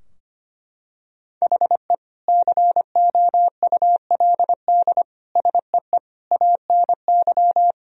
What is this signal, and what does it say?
Morse code, heard as a keyed tone, about 25 wpm, 700 Hz. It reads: HE COULD SEE ANY
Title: Sentences from Top 1000 Words - Review Part 3 - 25wpm